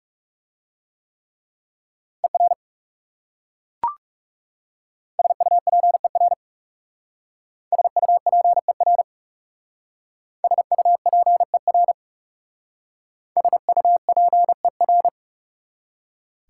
Morse code, heard as a keyed tone, about 37 wpm, 700 Hz. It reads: ER SUPER SUPER SUPER SUPER
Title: Speed-Racing Prefixes and Suffixes 30wpm